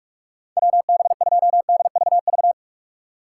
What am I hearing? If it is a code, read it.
Morse code, heard as a keyed tone, about 59 wpm, 700 Hz. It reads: WB2DVV